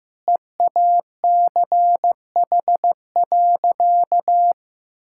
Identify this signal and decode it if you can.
Morse code, heard as a keyed tone, about 15 wpm, 700 Hz. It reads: EACH.
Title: Sentences from Top 1000 Words - Review Part 5 - 15wpm